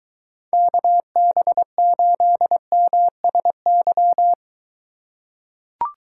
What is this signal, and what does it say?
Morse code, heard as a keyed tone, about 23 wpm, 700 Hz. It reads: KB8MSY